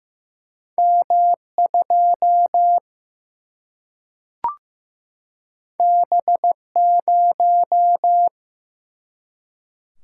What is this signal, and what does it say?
Morse code, heard as a keyed tone, about 15 wpm, 700 Hz. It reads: M 2 B 0
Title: Single-Letter Single-Number 15wpm